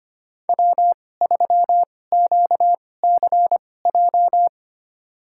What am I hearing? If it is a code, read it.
Morse code, heard as a keyed tone, about 25 wpm, spaced as for 18 wpm, 700 Hz. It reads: W3QCJ
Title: USA Call Signs - 2x Character Spacing - 25wpm